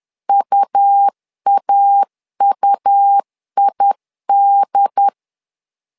Morse code, heard as a keyed tone, about 10 wpm, 800 Hz. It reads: UAUID
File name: LEZIONI / 2 UHDN / 2 UHDN 10wpm 1904